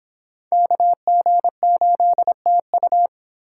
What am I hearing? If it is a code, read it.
Morse code, heard as a keyed tone, about 30 wpm, 700 Hz. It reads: KG8TU